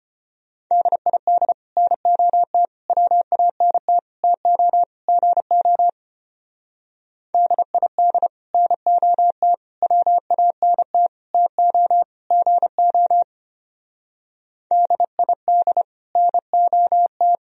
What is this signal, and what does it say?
Morse code, heard as a keyed tone, about 29 wpm, 700 Hz. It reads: DID NOT WANT TO GO DID NOT WANT TO GO DID NOT
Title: Speed-Racing - Sets of 5 Words 25wpm